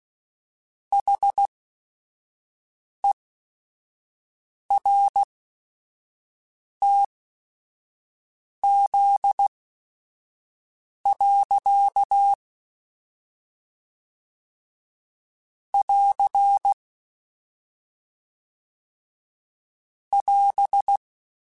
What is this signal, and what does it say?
Morse code, heard as a keyed tone, about 16 wpm, spaced as for 5 wpm, 800 Hz. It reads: HERTZ. <AR> <AS>
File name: Technical Subjects / Technical Subjects-mp3s / Reactance @5